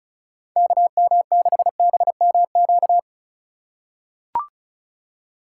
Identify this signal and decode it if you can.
Morse code, heard as a keyed tone, about 36 wpm, 700 Hz. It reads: KM6BMQ
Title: USA Call Signs - 3x Character Spacing - 35wpm